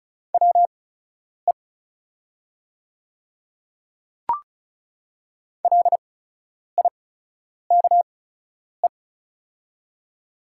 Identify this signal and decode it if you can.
Morse code, heard as a keyed tone, about 34 wpm, spaced as for 10 wpm, 690 Hz. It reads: WE LIKE
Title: 8x Character Spacing - Top 200 Words - 35wpm